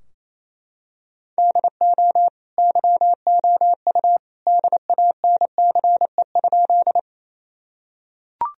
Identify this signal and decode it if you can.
Morse code, heard as a keyed tone, about 28 wpm, 700 Hz. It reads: DO YOU DANCE?
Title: Sentences from Top 1000 Words - Review Part 4 - 28wpm